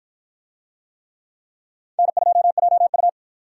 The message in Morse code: N2JU

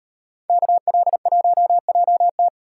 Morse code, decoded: KL1JT

